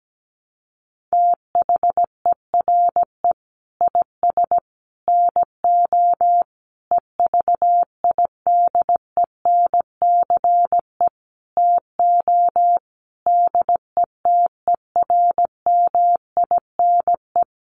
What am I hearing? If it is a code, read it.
Morse code, heard as a keyed tone, about 17 wpm, 700 Hz. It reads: THERE IS NO EVIDENCE TO DETERMINE
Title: Sentences from Top 1300 Words 17wpm